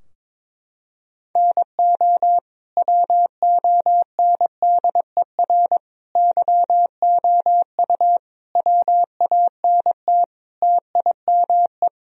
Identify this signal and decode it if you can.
Morse code, heard as a keyed tone, about 22 wpm, 700 Hz. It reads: NO WONDER YOU WANT TIME
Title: Sentences from Top 1000 Words - Review Part 3 - 22wpm